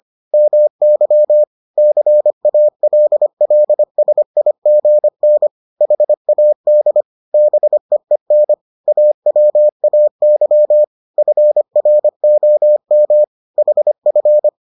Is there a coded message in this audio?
Yes: Morse code from 0.3 to 14.5 s, heard as a keyed tone, about 25 wpm, 600 Hz. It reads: MY CALLSIGN HAD BEEN AWAY FROM HF